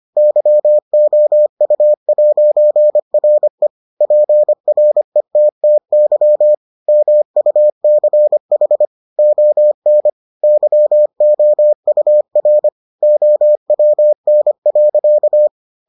Morse code, heard as a keyed tone, about 25 wpm, 600 Hz. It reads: YOU'RE PRETTY MUCH ON YOUR OWN.